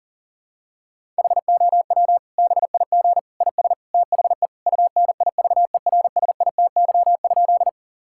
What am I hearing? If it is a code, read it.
Morse code, heard as a keyed tone, about 40 wpm, 700 Hz. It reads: HOW BIG IS THE UNIVERSITY?
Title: Sentences from Top 1000 Words - Review Part 5 - 40wpm